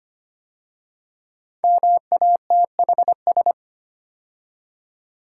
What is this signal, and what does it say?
Morse code, heard as a keyed tone, about 25 wpm, 700 Hz. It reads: MATHS
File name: 5 Letter Words - First 3 Letters Given - 25wpm